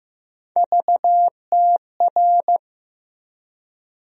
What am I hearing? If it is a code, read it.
Morse code, heard as a keyed tone, about 15 wpm, 700 Hz. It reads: VTR
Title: USA Call Sign Postfixes - 3 Letters 15wpm